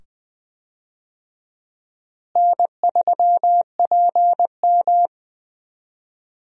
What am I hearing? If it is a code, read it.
Morse code, heard as a keyed tone, about 20 wpm, 700 Hz. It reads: N3PM
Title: USA Call Signs - 4 Characters - Repeated 2x - 20wpm